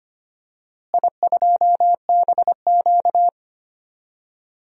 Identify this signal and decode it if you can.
Morse code, heard as a keyed tone, about 25 wpm, 700 Hz. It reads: I2BQ
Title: DXCC-100 Call Signs - 4 Characters - Repeated 3x - 25wpm